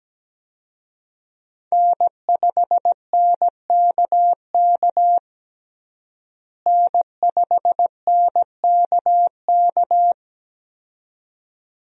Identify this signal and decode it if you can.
Morse code, heard as a keyed tone, about 17 wpm, 700 Hz. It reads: N5NKK N5NKK
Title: USA Call Signs - 5 Characters - Repeated 2x - 17wpm